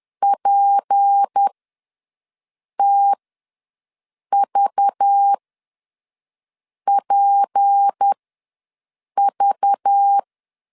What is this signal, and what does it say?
Morse code, heard as a keyed tone, about 10 wpm, spaced as for 5 wpm, 800 Hz. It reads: PTVPV
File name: LEZIONI / 6 PKX0 / 6 P ws 10wpm 3751